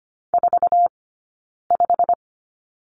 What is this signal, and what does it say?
Morse code, heard as a keyed tone, about 25 wpm, 700 Hz. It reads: 4 5
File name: Lesson 15d - Number 5 - Two Characters 25wpm